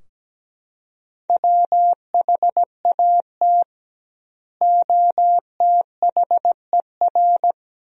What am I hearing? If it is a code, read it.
Morse code, heard as a keyed tone, about 17 wpm, 700 Hz. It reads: WHAT OTHER
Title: 2x Word-Spacing - Sentences from Top 100 Words - 17wpm